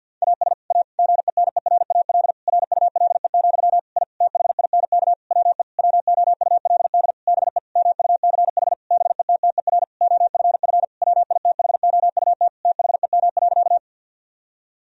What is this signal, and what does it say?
Morse code, heard as a keyed tone, about 50 wpm, 700 Hz. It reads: AS A GENERAL RULE, I THINK WE WOULD BE MUCH BETTER OFF WITHOUT THEM.
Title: Sentences from Top 1000 Words - Review Part 3 - 50wpm